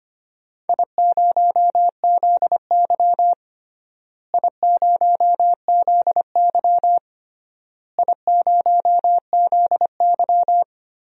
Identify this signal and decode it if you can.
Morse code, heard as a keyed tone, about 25 wpm, 700 Hz. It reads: I0ZY I0ZY I0ZY
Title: DXCC-100 Call Signs - 4 Characters - Repeated 3x - 25wpm